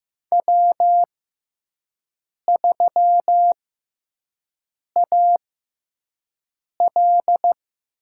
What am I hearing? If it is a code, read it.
Morse code, heard as a keyed tone, about 15 wpm, spaced as for 5 wpm, 700 Hz. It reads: W3AL
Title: USA Call Signs - 4 Characters - 6x Character Spacing - 15wpm